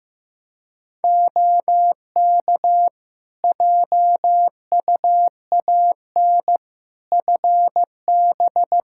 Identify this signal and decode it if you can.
Morse code, heard as a keyed tone, about 15 wpm, 700 Hz. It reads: OK JUAN FB
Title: K1USN SST QSO Practice 15wpm